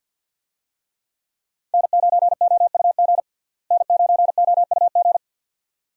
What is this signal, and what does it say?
Morse code, heard as a keyed tone, about 51 wpm, 700 Hz. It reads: N9OUG N9OUG
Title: USA Call Signs - 5 Characters - Repeated 2x - 50wpm